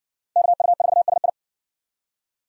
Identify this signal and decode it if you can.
Morse code, heard as a keyed tone, about 58 wpm, 700 Hz. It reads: KU4SI